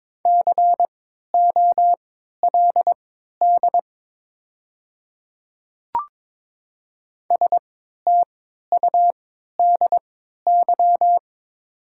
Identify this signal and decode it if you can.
Morse code, heard as a keyed tone, about 22 wpm, spaced as for 13 wpm, 700 Hz. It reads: COLD STUDY
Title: 3x Character Spacing - Top 500 Words - 22wpm